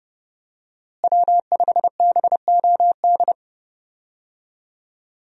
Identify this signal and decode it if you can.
Morse code, heard as a keyed tone, about 30 wpm, 700 Hz. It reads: W5BOD
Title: USA Call Signs - 5 Characters - Repeated 3x - 30wpm